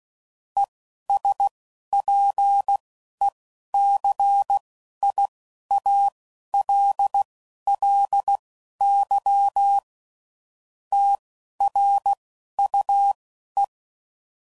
Morse code, heard as a keyed tone, about 16 wpm, spaced as for 11 wpm, 800 Hz. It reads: ESPECIALLY TRUE
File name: CW Code Groups @10WPM/Technical Subjects